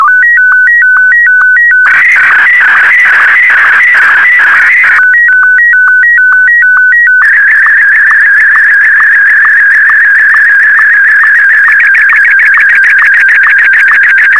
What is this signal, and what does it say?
SSTV (slow-scan television), Martin 1.